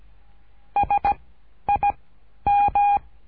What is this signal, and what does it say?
Morse code, heard as a keyed tone, about 17 wpm, spaced as for 10 wpm, 810 Hz. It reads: SIM
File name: cursos / CURSO DE TELEGRAFIA / Telegrafia - Trilha 13